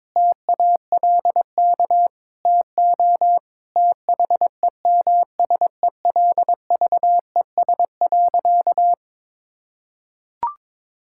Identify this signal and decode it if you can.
Morse code, heard as a keyed tone, about 22 wpm, 700 Hz. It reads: TALK TO THEMSELVES.
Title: Sentences from Top 1000 Words - Review Part 5 - 22wpm